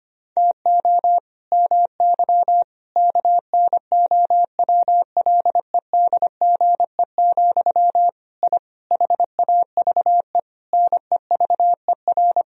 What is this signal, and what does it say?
Morse code, heard as a keyed tone, about 25 wpm, 700 Hz. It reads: TO MY KNOWLEDGE, I HAVE NEVER
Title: Sentences from Top 1000 Words - Review Part 3 - 25wpm